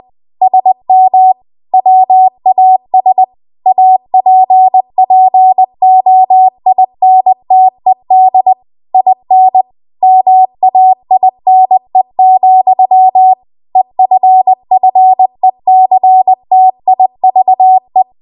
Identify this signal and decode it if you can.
Morse code, heard as a keyed tone, about 20 wpm, 750 Hz. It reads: SM WAS APPOINTED IN MAINE, EFFECTIVE
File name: Archive / 20 WPM / 20WPM